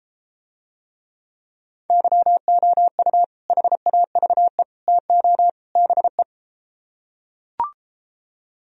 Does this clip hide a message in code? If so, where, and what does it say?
Yes: Morse code from 1.9 to 6.2 s, heard as a keyed tone, about 33 wpm, 700 Hz. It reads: YOU HAVE TO BE